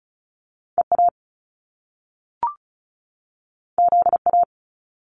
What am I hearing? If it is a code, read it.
Morse code, heard as a keyed tone, about 36 wpm, 700 Hz. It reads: EA ZA
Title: Top Two Letter Combinations in Words 35wpm